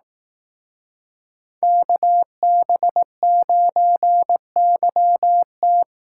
Morse code, heard as a keyed tone, about 18 wpm, 700 Hz. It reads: KB9YT
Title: Speed Racing - USA 2x2 Call Signs 15wpm